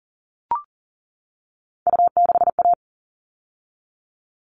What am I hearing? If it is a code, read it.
Morse code, heard as a keyed tone, about 40 wpm, 700 Hz. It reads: U6A
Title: All Cut Numbers - 3-Digit 40wpm